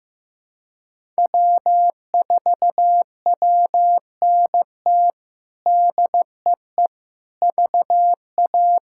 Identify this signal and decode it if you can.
Morse code, heard as a keyed tone, about 16 wpm, 700 Hz. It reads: W4WNT DEE VA
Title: K1USN SST QSO Practice - 15wpm